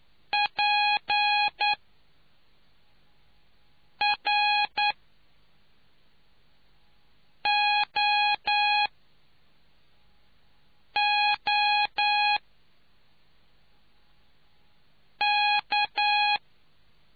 Morse code, heard as a keyed tone, about 9 wpm, spaced as for 3 wpm, 830 Hz. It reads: PROOK